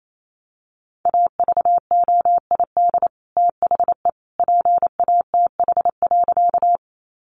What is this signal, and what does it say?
Morse code, heard as a keyed tone, about 28 wpm, 700 Hz. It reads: AVOID THE PATH.
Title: Sentences from Top 1400 Words 28wpm